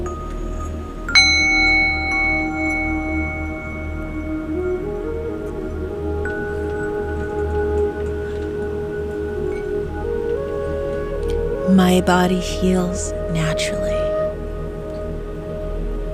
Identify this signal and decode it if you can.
Dial-up modem handshake, with speech.